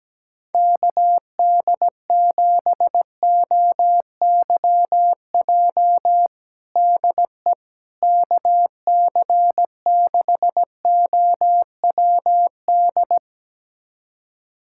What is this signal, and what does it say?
Morse code, heard as a keyed tone, about 17 wpm, 700 Hz. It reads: KD7OYJ DE KC6OWD
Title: USA Callsign DE Callsign - 6 Characters - 17wpm